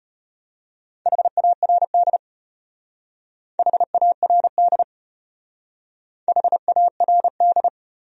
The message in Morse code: HARD HARD HARD

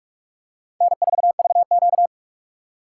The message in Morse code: N4VQ

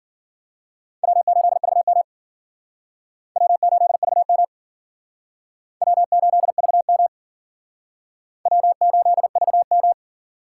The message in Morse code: W8VM W8VM W8VM W8VM

